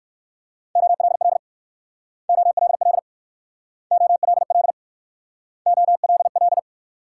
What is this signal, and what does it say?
Morse code, heard as a keyed tone, about 58 wpm, 690 Hz. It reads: OLL OLL OLL OLL